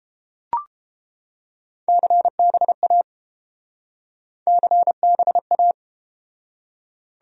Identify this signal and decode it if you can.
Morse code, heard as a keyed tone, about 30 wpm, 700 Hz. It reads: CBA CBA